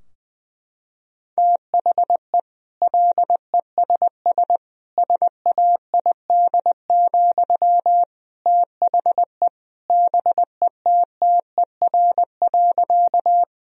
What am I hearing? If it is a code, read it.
Morse code, heard as a keyed tone, about 20 wpm, 700 Hz. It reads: THE LESS SAID, THE BETTER.
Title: Sentences from Top 1000 Words - Review Part 2 - 20wpm